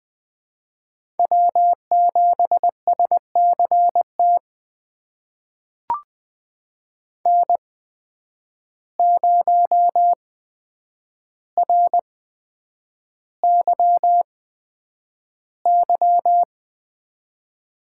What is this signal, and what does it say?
Morse code, heard as a keyed tone, about 20 wpm, spaced as for 9 wpm, 700 Hz. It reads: W7SCT N0RYY
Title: USA Call Signs - 5 Characters - 8x Character Spacing - 20wpm